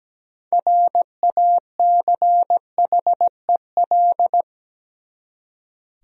Morse code, heard as a keyed tone, about 17 wpm, 700 Hz. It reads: RACHEL
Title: USA Common Names - 17wpm